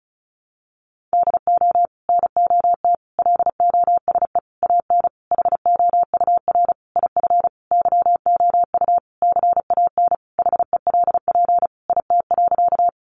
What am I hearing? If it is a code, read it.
Morse code, heard as a keyed tone, about 34 wpm, 700 Hz. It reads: DO NOT LOSE AN HOUR IF YOU CAN HELP IT.
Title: Sentences from Top 1100 Words 35wpm